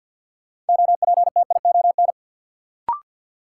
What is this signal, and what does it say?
Morse code, heard as a keyed tone, about 52 wpm, 700 Hz. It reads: OPTION
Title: Sentences from Top 1300 Words - New Word Drill - 50wpm